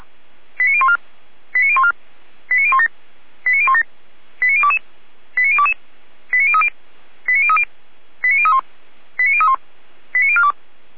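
Five-tone selective call (ZVEI), heard as quick runs of five tones, tones of about 70 ms. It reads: 89014 89014 89017 89017 89022 89022 89030 89030 89031 89031 89042
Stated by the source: ZVEI five-tone call 89011 received on an SDR.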